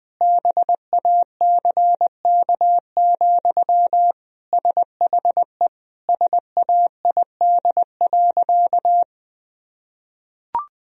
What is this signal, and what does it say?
Morse code, heard as a keyed tone, about 20 wpm, 700 Hz. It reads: BACK, SHE SAID.
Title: Sentences from Top 1000 Words - Review Part 2 - 20wpm